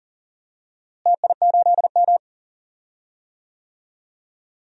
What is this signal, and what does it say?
Morse code, heard as a keyed tone, about 40 wpm, 700 Hz. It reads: TI8M
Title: DXCC-100 Call Signs - 4 Characters - Repeated 3x - 40wpm